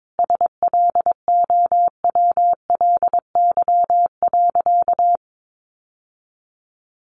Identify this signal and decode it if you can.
Morse code, heard as a keyed tone, about 22 wpm, 700 Hz. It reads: SLOWLY.
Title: Sentences from Top 1100 Words 22wpm